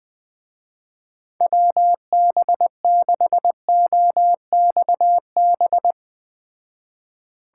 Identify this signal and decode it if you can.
Morse code, heard as a keyed tone, about 20 wpm, 700 Hz. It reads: WB6OXB